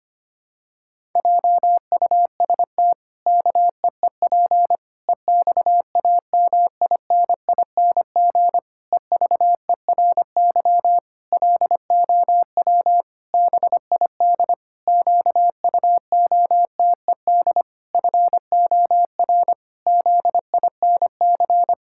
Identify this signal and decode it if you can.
Morse code, heard as a keyed tone, about 25 wpm, 700 Hz. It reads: JUST KEEP EXAMINING EVERY LOW BID QUOTED FOR ZINC
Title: Pangram Challenge v2 - 25wpm